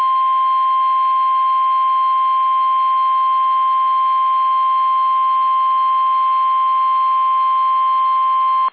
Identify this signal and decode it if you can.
Emergency Alert System (SAME) tones (the 1050 Hz attention tone). It is heard from 0.2 to 7.9 s.
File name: eas / KUSO / KUSO